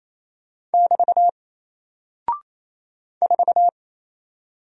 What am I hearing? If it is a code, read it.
Morse code, heard as a keyed tone, about 27 wpm, 700 Hz. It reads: <BT> 4
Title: Single Character (Letter:Number:Punctuation:Prosign) Alt 28wpm